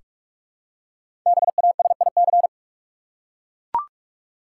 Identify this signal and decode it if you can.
Morse code, heard as a keyed tone, about 43 wpm, 700 Hz. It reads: BASIC
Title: Five Letter Words - 45wpm